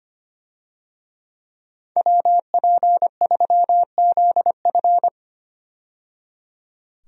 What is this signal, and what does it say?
Morse code, heard as a keyed tone, about 25 wpm, 700 Hz. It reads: WP3ZF
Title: USA Call Signs - 5 Characters - 25wpm